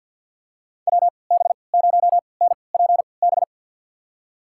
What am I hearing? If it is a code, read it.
Morse code, heard as a keyed tone, about 52 wpm, spaced as for 29 wpm, 700 Hz. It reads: WB0NPB
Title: USA Call Signs - 6 Characters - 3x Character Spacing - 50wpm